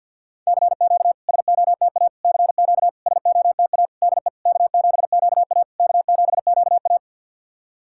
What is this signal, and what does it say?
Morse code, heard as a keyed tone, about 52 wpm, 690 Hz. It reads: CQ SOTA CQ SOTA DE K7QA K7QA
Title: SOTA QSO Practice 50wpm